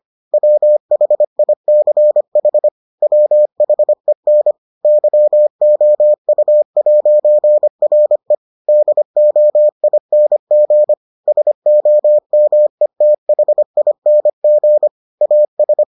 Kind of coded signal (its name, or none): Morse code